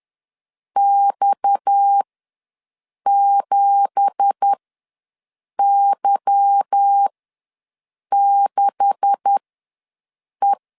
Morse code, heard as a keyed tone, about 10 wpm, spaced as for 6 wpm, 800 Hz. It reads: X7Y6E
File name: LEZIONI / 9 2867 ws 10wpm 4215